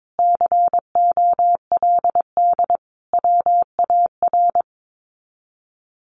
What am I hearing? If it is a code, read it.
Morse code, heard as a keyed tone, about 22 wpm, 700 Hz. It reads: COLD WAR
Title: Sets of 5 Words Encore 22wpm